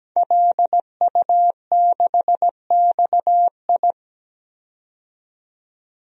Morse code, heard as a keyed tone, about 17 wpm, 700 Hz. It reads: LU6XI
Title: DXCC-100 Call Signs - 5 Characters - 17wpm